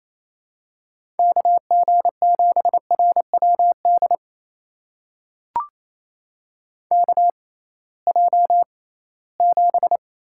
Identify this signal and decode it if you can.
Morse code, heard as a keyed tone, about 28 wpm, spaced as for 18 wpm, 700 Hz. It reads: KG7RWD KJ7
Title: USA Call Signs - 6 Characters - 6x Character Spacing - 28wpm